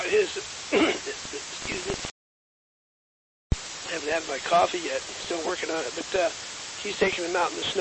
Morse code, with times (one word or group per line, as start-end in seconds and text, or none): none